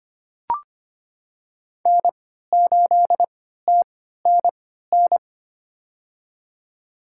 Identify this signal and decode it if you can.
Morse code, heard as a keyed tone, about 25 wpm, spaced as for 14 wpm, 700 Hz. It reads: N8TNN
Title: USA Call Signs - 5 Characters - 3x Character Spacing - 25wpm